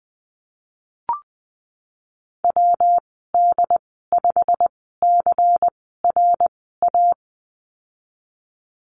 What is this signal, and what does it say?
Morse code, heard as a keyed tone, about 20 wpm, spaced as for 15 wpm, 700 Hz. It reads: WD5CRA
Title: 128 Lesson 21g - Number 6 - Call Signs 20wpm